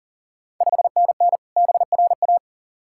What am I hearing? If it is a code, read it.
Morse code, heard as a keyed tone, about 40 wpm, 690 Hz. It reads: 5NN BRA
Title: Ontario QSO Party Exchange 40wpm